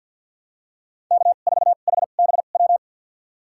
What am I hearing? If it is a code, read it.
Morse code, heard as a keyed tone, about 53 wpm, spaced as for 37 wpm, 700 Hz. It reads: K4HBW
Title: USA Call Signs - 5 Characters - 2x Character Spacing - 50wpm